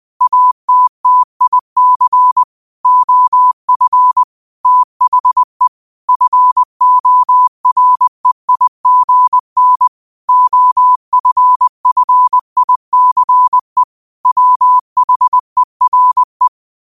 Morse code, heard as a keyed tone, about 20 wpm, 1000 Hz. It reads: ATTIC OF THE FOREIGN OFFICE WHERE
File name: qotdfiles / 20wpm